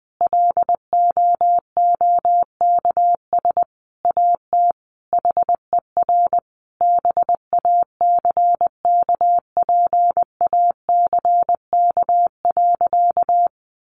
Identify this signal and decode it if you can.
Morse code, heard as a keyed tone, about 20 wpm, 700 Hz. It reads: LOOKS AT HER BACKPACK.
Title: Additional Easy Phrases 20wpm